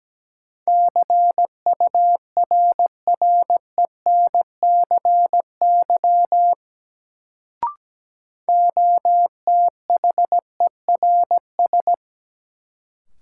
Morse code, heard as a keyed tone, about 17 wpm, 700 Hz. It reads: CURRENCY OTHERS
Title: Sentences from Top 1500 Words - New Word Drill Alt - 17wpm